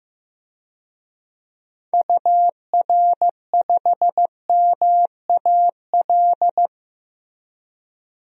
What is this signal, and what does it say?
Morse code, heard as a keyed tone, about 15 wpm, 700 Hz. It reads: UR5MAL
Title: DXCC-100 Call Signs - 6 Characters - Repeated 3x - 15wpm